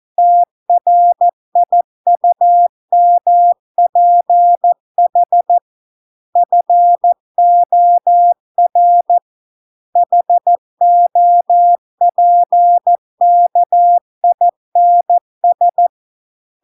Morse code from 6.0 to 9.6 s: FOR